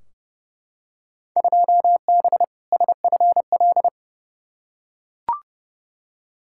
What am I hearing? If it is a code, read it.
Morse code, heard as a keyed tone, about 30 wpm, 700 Hz. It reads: 2B SFL
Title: Field Day Exchanges Encore - 30wpm